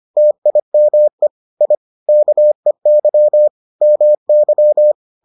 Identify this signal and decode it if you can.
Morse code, heard as a keyed tone, about 25 wpm, 600 Hz. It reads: TIME I KEY MY